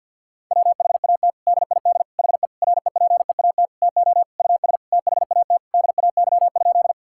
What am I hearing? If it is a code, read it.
Morse code, heard as a keyed tone, about 50 wpm, 700 Hz. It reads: WHAT DID HE REPEAT TO US THAT DAY?
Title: Sentences from Top 1600 Words 50wpm